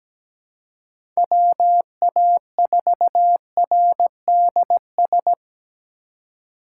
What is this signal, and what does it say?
Morse code, heard as a keyed tone, about 17 wpm, 700 Hz. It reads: WA4RDS
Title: USA Call Signs - 6x Character Spacing - 17wpm